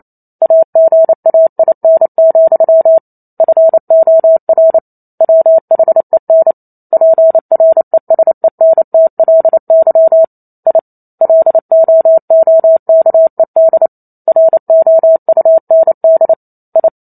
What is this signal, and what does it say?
Morse code, heard as a keyed tone, about 28 wpm, 650 Hz. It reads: AGAIN, FOR WHEN PRESENTLY I LOOKED ROUND I